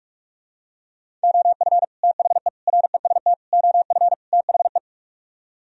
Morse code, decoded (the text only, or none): OF THE REST OF THE